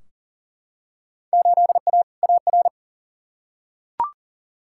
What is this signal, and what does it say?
Morse code, heard as a keyed tone, about 40 wpm, 700 Hz. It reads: 8A AR
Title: Field Day Exchanges - 40wpm